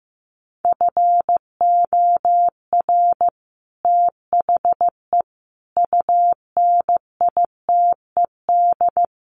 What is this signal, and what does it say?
Morse code, heard as a keyed tone, about 15 wpm, 700 Hz. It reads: FOR THE UNITED
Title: Sets of 5 Words Encore 15wpm